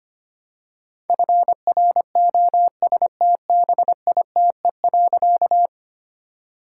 Morse code, heard as a keyed tone, about 25 wpm, 700 Hz. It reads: FROSTBITE.
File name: practice / 25wpm / All-Age Appropriate Jokes - 25wpm